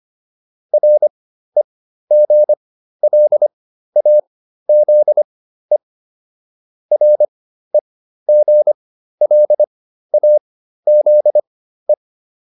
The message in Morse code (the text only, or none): REGLAZE REGLAZE